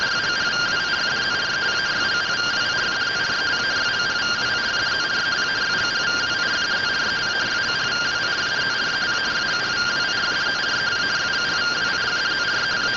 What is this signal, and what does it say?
Two-tone teleprinter signal, about 170 Hz shift.
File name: findings / 100 surview 0.gen-12 rtty rx